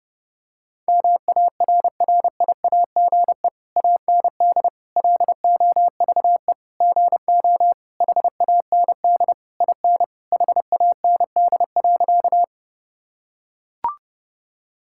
Morse code, heard as a keyed tone, about 30 wpm, 700 Hz. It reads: MARRIAGE AND LOVE GO HAND IN HAND.
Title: Sentences from Top 1000 Words - Review Part 5 - 30wpm